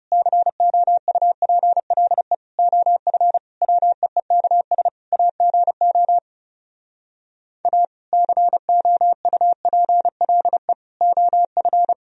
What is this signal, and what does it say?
Morse code, heard as a keyed tone, about 40 wpm, 700 Hz. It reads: COUPLE OF WEEKS AGO A COUPLE OF